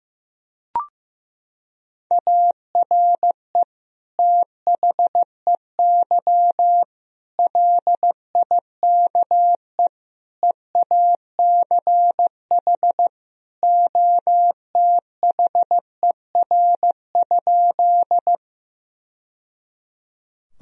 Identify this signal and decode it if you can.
Morse code, heard as a keyed tone, about 15 wpm, 700 Hz. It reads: ARE THEY LIKE EACH OTHER?
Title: Sentences from Top 1000 Words - Review Part 1 - 15wpm